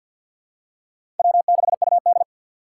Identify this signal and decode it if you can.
Morse code, heard as a keyed tone, about 49 wpm, 700 Hz. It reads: W6UD
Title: USA Call Signs - 4 Characters - 3x Character Spacing - 50wpm